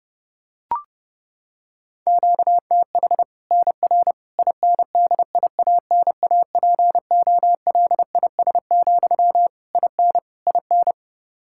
Morse code, heard as a keyed tone, about 29 wpm, 700 Hz. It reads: QTH NR INDIANAPOLIS, IN IN
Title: Practice QSO - Exchange 1 - Beginning - 30wpm